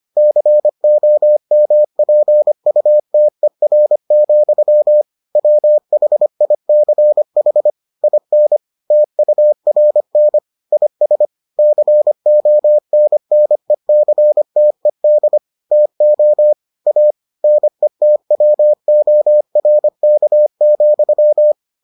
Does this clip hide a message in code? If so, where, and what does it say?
Yes: Morse code from 0.2 to 21.5 s, heard as a keyed tone, about 25 wpm, 600 Hz. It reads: COMPUTER, WHICH IN TURN IS CONNECTED TO A NETWORK,